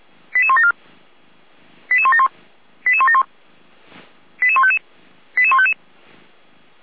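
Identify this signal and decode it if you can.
Five-tone selective call (ZVEI), heard as quick runs of five tones, tones of about 70 ms. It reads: 88174 88171 88171 88155 88155
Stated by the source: ZVEI five-tone call received on SDR, logged by Feuerwehr Orlamünde.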